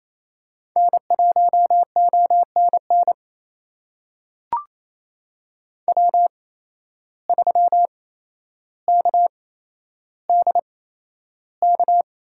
Morse code, heard as a keyed tone, about 28 wpm, spaced as for 12 wpm, 700 Hz. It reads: N1ONN W3KDK
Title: USA Call Signs - 5 Characters - 8x Character Spacing - 28wpm